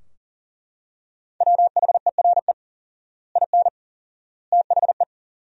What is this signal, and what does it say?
Morse code, heard as a keyed tone, about 37 wpm, 700 Hz. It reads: WHERE IN THE
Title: 4x Word-Spacing - Sentences from Top 200 Words - 40wpm